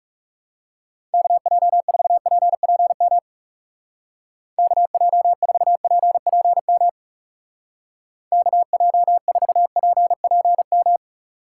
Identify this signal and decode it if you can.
Morse code, heard as a keyed tone, about 37 wpm, 700 Hz. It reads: KJ4PPM KJ4PPM KJ4PPM
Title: Speed Racing - USA Call Signs 30wpm